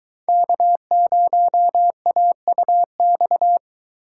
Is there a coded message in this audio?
Yes: Morse code from 0.3 to 3.6 s, heard as a keyed tone, about 23 wpm, 700 Hz. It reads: K0AUX